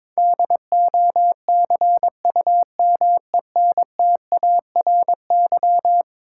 Morse code, heard as a keyed tone, about 22 wpm, 700 Hz. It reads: DOCUMENTARY
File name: practice / 22wpm / Eleven Letter Words - 22wpm